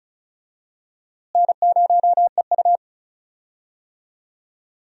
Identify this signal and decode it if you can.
Morse code, heard as a keyed tone, about 36 wpm, 700 Hz. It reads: N0EU
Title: USA Call Signs - 4 Characters - Repeated 3x - 35wpm